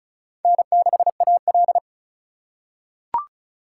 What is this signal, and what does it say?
Morse code, heard as a keyed tone, about 36 wpm, 700 Hz. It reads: N6AL
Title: USA Call Signs - 4 Characters - 3x Character Spacing - 35wpm